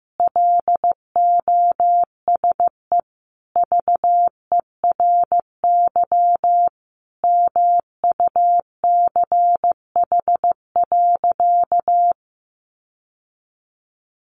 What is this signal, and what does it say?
Morse code, heard as a keyed tone, about 15 wpm, 700 Hz. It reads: LOSE VERY MUCH.
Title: Sentences from Top 1100 Words 15wpm